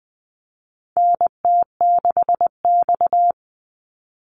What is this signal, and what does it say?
Morse code, heard as a keyed tone, about 20 wpm, 700 Hz. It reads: NT6X
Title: Call Signs Repeated 3x - 20wpm